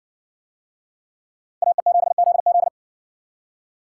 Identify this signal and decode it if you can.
Morse code, heard as a keyed tone, about 60 wpm, 700 Hz. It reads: AE7ZZ